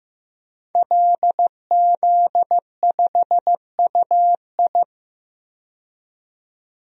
Morse code, heard as a keyed tone, about 15 wpm, 700 Hz. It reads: LZ5UI
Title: DXCC-100 Call Signs - 5 Characters - 15wpm